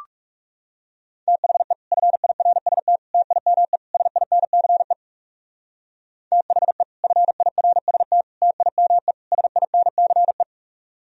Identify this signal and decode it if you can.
Morse code, heard as a keyed tone, about 36 wpm, 700 Hz. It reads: THE FIRST TIME SINCE THE FIRST TIME SINCE
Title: Speed-Racing - Sets of 4 Words 30wpm